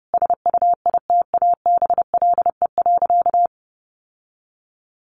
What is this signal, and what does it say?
Morse code, heard as a keyed tone, about 30 wpm, 700 Hz. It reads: SUITABLE.
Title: Sentences from Top 1400 Words 30wpm